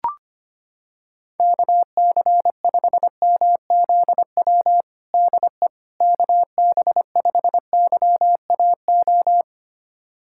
Morse code, heard as a keyed tone, about 25 wpm, 700 Hz. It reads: KC5MZW DE KB5YAO
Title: USA Callsign DE Callsign - 6 Characters - 25wpm